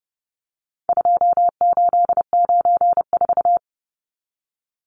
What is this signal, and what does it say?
Morse code, heard as a keyed tone, about 29 wpm, 700 Hz. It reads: 2894
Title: Numbers - 4-Digit 30wpm